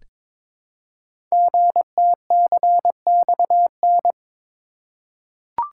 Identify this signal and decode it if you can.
Morse code, heard as a keyed tone, about 22 wpm, 700 Hz. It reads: GTCXN
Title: Code Groups - 5 Letters - 22wpm